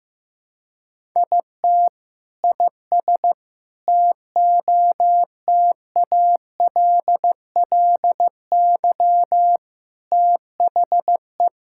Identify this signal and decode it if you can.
Morse code, heard as a keyed tone, about 16 wpm, 700 Hz. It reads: IT IS TOTALLY THE
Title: Sentences from Top 1600 Words 15wpm